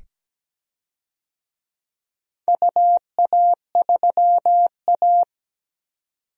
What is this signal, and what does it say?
Morse code, heard as a keyed tone, about 17 wpm, 700 Hz. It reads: UA3A
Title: DXCC-100 Call Signs - 4 Characters - 17wpm